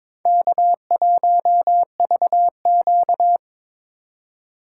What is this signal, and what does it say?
Morse code, heard as a keyed tone, about 26 wpm, 700 Hz. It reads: K1VQ
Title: Speed Racing - USA 1x2 Call Signs 22wpm